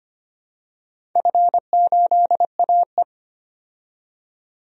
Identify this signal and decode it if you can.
Morse code, heard as a keyed tone, about 25 wpm, 700 Hz. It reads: F8AE
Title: DXCC-100 Call Signs - 4 Characters - 25wpm